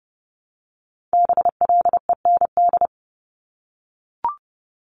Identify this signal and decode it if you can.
Morse code, heard as a keyed tone, about 30 wpm, 700 Hz. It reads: BLEND
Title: Multi-letter Phonemes Repeated 3x 30wpm